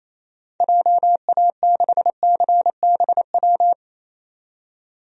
Morse code, heard as a keyed tone, about 27 wpm, 700 Hz. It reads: JA6CBW
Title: DXCC-100 Call Signs - 6 Characters - 28wpm